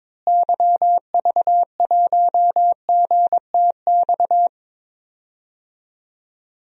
Morse code, heard as a keyed tone, about 22 wpm, 700 Hz. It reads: YV1GTX
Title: DXCC-100 Call Signs - 6 Characters - Repeated 2x - 22wpm